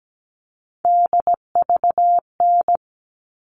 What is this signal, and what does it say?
Morse code, heard as a keyed tone, about 17 wpm, 700 Hz. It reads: DVN
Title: All Cut Numbers - 3-Digit 17wpm